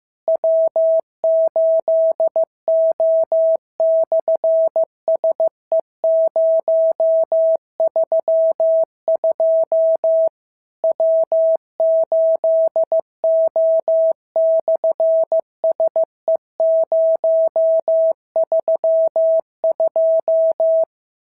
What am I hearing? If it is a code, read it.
Morse code, heard as a keyed tone, about 15 wpm, 650 Hz. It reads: W8O/SE032 W8O/SE032
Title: SOTA QSO Practice 15wpm